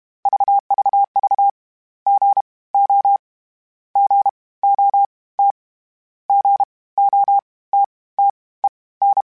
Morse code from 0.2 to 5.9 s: VVV GO GOT

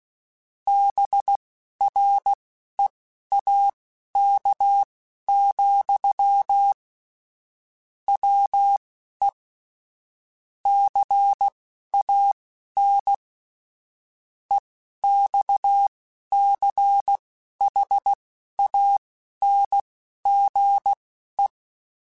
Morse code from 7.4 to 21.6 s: WE CAN EXCHANGE

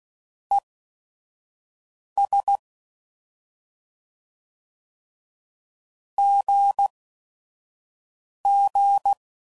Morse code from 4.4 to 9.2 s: GG

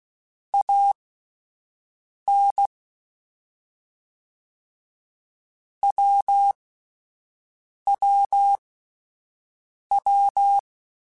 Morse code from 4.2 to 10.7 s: WWW

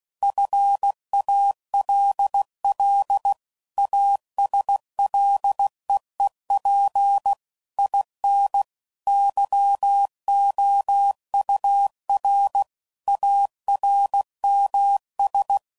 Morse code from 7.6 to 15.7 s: IN YOUR ARMS

